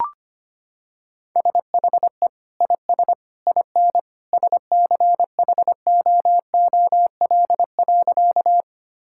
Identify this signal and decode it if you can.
Morse code, heard as a keyed tone, about 25 wpm, 690 Hz. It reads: SHE IS IN SCHOOL.
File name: practice / 25wpm / Sentences with 2-Word Phrases - 25wpm